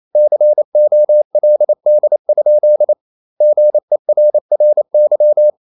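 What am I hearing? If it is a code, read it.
Morse code, heard as a keyed tone, about 28 wpm, 600 Hz. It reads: COLD? GERRY
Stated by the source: Morse code practice at 28 wpm, text of Ken Follett's novel Never in English, from CW med Gnister.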